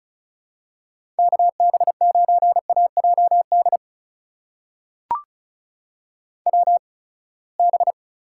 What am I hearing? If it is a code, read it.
Morse code, heard as a keyed tone, about 36 wpm, spaced as for 24 wpm, 700 Hz. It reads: KB9AJD WB
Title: USA Call Signs - 8x Character Spacing - 35wpm